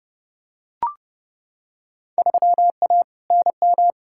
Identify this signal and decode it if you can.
Morse code, heard as a keyed tone, about 31 wpm, 700 Hz. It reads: 3A NM
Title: Field Day Exchanges Encore - 30wpm